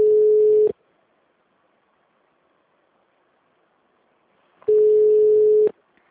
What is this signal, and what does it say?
Ringback tone (Europe, 425 Hz).